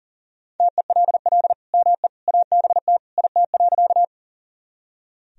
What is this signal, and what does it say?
Morse code, heard as a keyed tone, about 40 wpm, 700 Hz. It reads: TELL ME ABT IT.
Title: Sentences from Top 200 Words and Abbreviations 40wpm